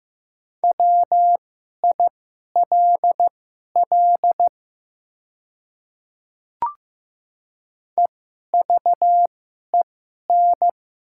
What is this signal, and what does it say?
Morse code, heard as a keyed tone, about 15 wpm, spaced as for 11 wpm, 700 Hz. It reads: WILL EVEN
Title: 2x Character Spacing - Top 300 Words - 15wpm